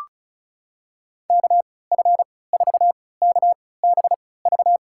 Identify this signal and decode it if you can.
Morse code, heard as a keyed tone, about 36 wpm, spaced as for 20 wpm, 700 Hz. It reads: KF4KBV
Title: USA Call Signs - 6 Characters - 3x Character Spacing - 35wpm